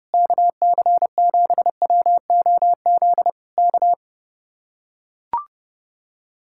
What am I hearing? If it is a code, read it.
Morse code, heard as a keyed tone, about 30 wpm, 700 Hz. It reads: KC7WOZ K